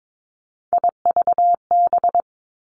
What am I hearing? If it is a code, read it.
Morse code, heard as a keyed tone, about 22 wpm, 700 Hz. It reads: IVB
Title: Three Challenging Letter-Number Combinations 22wpm